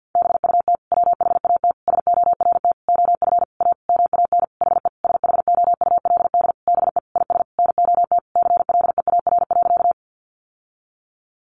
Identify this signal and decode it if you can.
Morse code, heard as a keyed tone, about 50 wpm, 700 Hz. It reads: BUT WHAT SORT OF A MAN HE SHOULD BE IS NOT CLEAR.
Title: Sentences from Top 1300 Words 50wpm